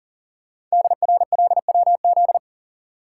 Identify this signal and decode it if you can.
Morse code, heard as a keyed tone, about 40 wpm, 700 Hz. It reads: DRLWZ
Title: Code Groups - 5 Letters - 40wpm